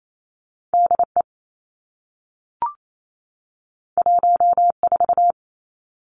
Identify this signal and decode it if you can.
Morse code, heard as a keyed tone, about 28 wpm, 700 Hz. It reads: D E 1 4
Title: Two Challenging Letter-Number Combinations 28wpm